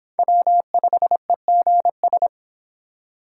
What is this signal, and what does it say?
Morse code, heard as a keyed tone, about 29 wpm, 700 Hz. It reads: W5EGS